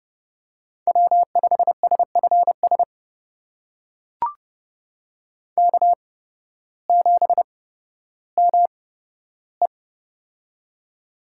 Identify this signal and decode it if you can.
Morse code, heard as a keyed tone, about 30 wpm, spaced as for 14 wpm, 700 Hz. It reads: W5SFS K7ME
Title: USA Call Signs - 8x Character Spacing - 30wpm